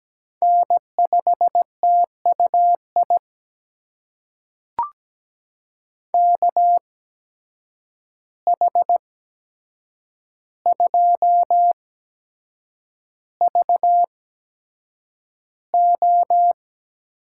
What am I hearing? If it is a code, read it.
Morse code, heard as a keyed tone, about 17 wpm, spaced as for 7 wpm, 700 Hz. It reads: N5TUI KH2VO
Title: USA Call Signs - 5 Characters - 8x Character Spacing - 17wpm